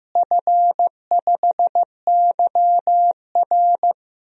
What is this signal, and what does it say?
Morse code, heard as a keyed tone, about 15 wpm, 700 Hz. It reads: F5YR
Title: DXCC-100 Call Signs - 4 Characters - Repeated 2x - 15wpm